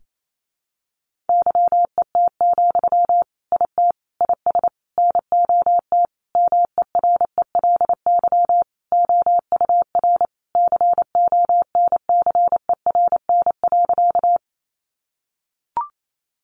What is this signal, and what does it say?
Morse code, heard as a keyed tone, about 28 wpm, 700 Hz. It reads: YET, IT IS NOT MERELY OUR CONCERN.
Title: Sentences from Top 1400 Words 28wpm